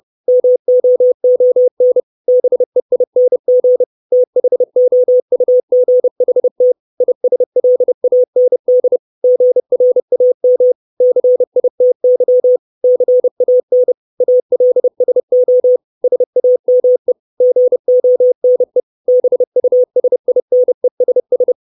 Morse code, heard as a keyed tone, about 30 wpm, 500 Hz. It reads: MOON BEING THOUGHT ISLAND GRAM CITY CAN ALSO SAME GONE BUSINESS
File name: 632 Words 30WPM